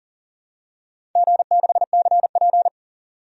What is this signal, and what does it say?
Morse code, heard as a keyed tone, about 40 wpm, 700 Hz. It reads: G6CP
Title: DXCC-100 Call Signs - 4 Characters - 40wpm